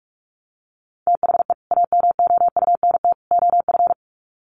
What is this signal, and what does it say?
Morse code, heard as a keyed tone, about 44 wpm, 700 Hz. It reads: THE AMOUNT OF
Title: Sets of 3 Words 45wpm